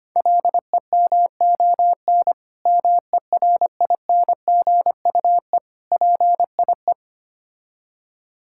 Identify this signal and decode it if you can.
Morse code, heard as a keyed tone, about 25 wpm, 700 Hz. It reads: LEMON MERINGUE PIE
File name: practice / 25wpm / Difficult Words with Hints - 25wpm